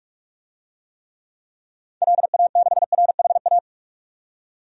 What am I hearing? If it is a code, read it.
Morse code, heard as a keyed tone, about 46 wpm, 700 Hz. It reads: LA6RHA
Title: DXCC-100 Call Signs - 6 Characters - 45wpm